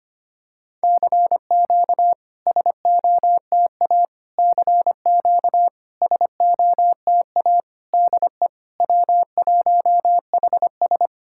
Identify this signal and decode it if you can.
Morse code, heard as a keyed tone, about 25 wpm, 700 Hz. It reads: CQ SOTA CQ SOTA DE W1HS
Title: SOTA QSO Practice 25wpm